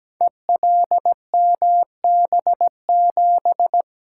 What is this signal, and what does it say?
Morse code, heard as a keyed tone, about 17 wpm, 700 Hz. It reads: ELMB7
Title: Code Groups - 5 Letter:Numbers - 17wpm